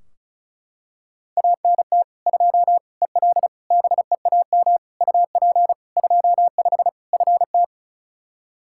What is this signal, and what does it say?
Morse code, heard as a keyed tone, about 36 wpm, 700 Hz. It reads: ANT 2 EL BEAM UP 25 FT
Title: Practice QSO - Exchange 2 - Info - 35wpm